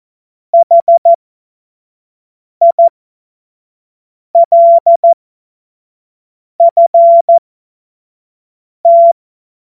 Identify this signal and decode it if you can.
Morse code, heard as a keyed tone, about 14 wpm, spaced as for 5 wpm, 680 Hz. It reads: HILFT
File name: Morsekurs / Witze / Witz 13. Juli 2021, 25 F70